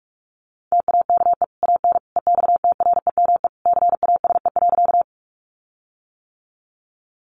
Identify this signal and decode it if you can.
Morse code, heard as a keyed tone, about 44 wpm, 700 Hz. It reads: TAKE AN EXTREME CASE.
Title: Sentences from Top 1300 Words 45wpm